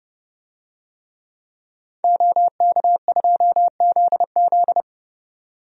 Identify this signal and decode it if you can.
Morse code, heard as a keyed tone, about 30 wpm, 700 Hz. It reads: OK2ZZ
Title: DXCC-100 Call Signs - 5 Characters - 30wpm